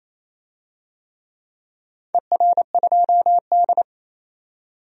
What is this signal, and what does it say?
Morse code, heard as a keyed tone, about 28 wpm, 700 Hz. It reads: ER2D
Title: DXCC-100 Call Signs - 4 Characters - 28wpm